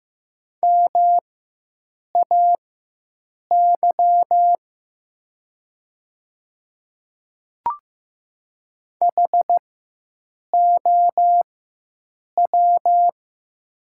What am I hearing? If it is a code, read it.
Morse code, heard as a keyed tone, about 15 wpm, spaced as for 7 wpm, 700 Hz. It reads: MAY HOW
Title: 4x Character Spacing - Top 100 Words - 15wpm